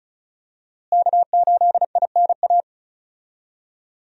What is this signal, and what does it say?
Morse code, heard as a keyed tone, about 34 wpm, 700 Hz. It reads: K8INA
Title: USA Call Signs - 5 Characters - Repeated 2x - 35wpm